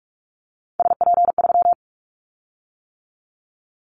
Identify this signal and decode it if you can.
Morse code, heard as a keyed tone, about 44 wpm, 700 Hz. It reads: SP3
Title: Callsign Prefixes for Top 25 DXCC - 45wpm